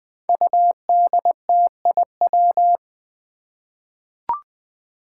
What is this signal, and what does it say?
Morse code, heard as a keyed tone, about 21 wpm, 700 Hz. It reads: UDTIW
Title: Five Challenging Letter-Number Combinations 20wpm